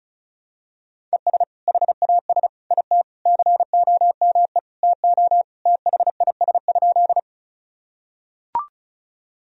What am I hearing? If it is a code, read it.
Morse code, heard as a keyed tone, about 34 wpm, 700 Hz. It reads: ES HAS IT COME TO THIS?
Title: Sentences from Top 100 Words and Abbreviations 35wpm